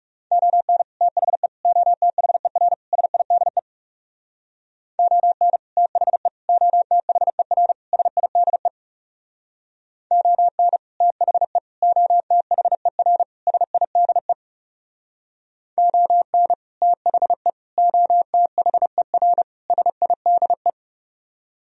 Morse code, heard as a keyed tone, about 40 wpm, 700 Hz. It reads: ON THE OTHER SIDE ON THE OTHER SIDE ON THE OTHER SIDE ON THE OTHER SIDE